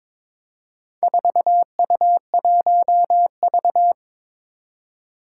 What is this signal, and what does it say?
Morse code, heard as a keyed tone, about 22 wpm, 700 Hz. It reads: 4U1V
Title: Four Challenging Letter-Number Combinations 22wpm